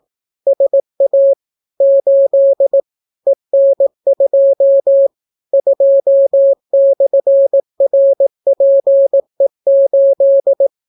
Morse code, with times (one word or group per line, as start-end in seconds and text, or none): 0.5-1.3 SA
1.8-2.8 8
3.3-5.1 EN2
5.5-10.7 2/RPE8